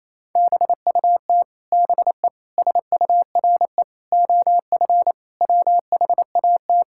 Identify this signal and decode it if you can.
Morse code, heard as a keyed tone, about 28 wpm, 700 Hz. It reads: BUT BE SURE OF WHAT
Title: Sentences from Top 1600 Words 28wpm